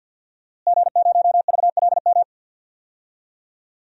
Morse code, heard as a keyed tone, about 50 wpm, 700 Hz. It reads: G0VLM
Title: DXCC-100 Call Signs - 5 Characters - Repeated 2x - 50wpm